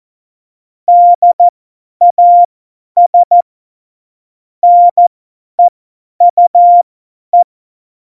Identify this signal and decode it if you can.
Morse code, heard as a keyed tone, about 14 wpm, spaced as for 10 wpm, 700 Hz. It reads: DAS NEUE
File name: Morsekurs / Witze / Witz 15. Januar 2019, 50 F70